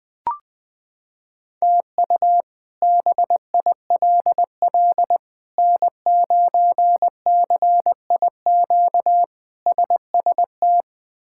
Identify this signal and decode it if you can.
Morse code, heard as a keyed tone, about 21 wpm, 700 Hz. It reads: TU BILL N9CIQ SST
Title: K1USN SST QSO Practice - 20wpm